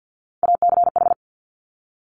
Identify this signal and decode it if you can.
Morse code, heard as a keyed tone, about 50 wpm, 700 Hz. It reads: ACH
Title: Top Three Letter Combinations in Words 50wpm